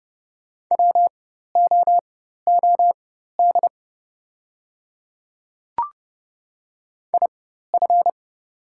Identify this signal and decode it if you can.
Morse code, heard as a keyed tone, about 30 wpm, spaced as for 14 wpm, 700 Hz. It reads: WOOD IF
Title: 4x Character Spacing - Top 300 Words - 30wpm